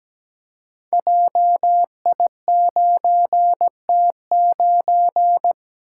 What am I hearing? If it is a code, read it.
Morse code, heard as a keyed tone, about 17 wpm, 700 Hz. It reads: JI9T9
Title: Code Groups - 5 Letter:Numbers - 17wpm